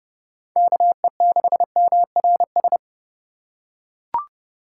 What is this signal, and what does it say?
Morse code, heard as a keyed tone, about 30 wpm, 700 Hz. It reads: KE6MRS
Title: USA Call Signs - 6 Characters - 6x Character Spacing - 30wpm